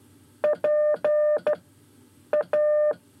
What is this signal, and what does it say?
Morse code, heard as a keyed tone, about 13 wpm, 600 Hz. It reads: P A
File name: 5 Lezione CW 1